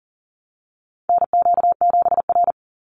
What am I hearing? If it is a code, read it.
Morse code, heard as a keyed tone, about 40 wpm, 700 Hz. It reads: NQ7R
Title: NA SOTA Chaser Callsigns 40wpm